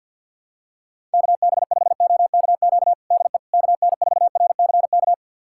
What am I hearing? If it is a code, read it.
Morse code, heard as a keyed tone, about 50 wpm, 700 Hz. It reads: KB5OKQ DE KN4RCK
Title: USA Callsign DE Callsign 50wpm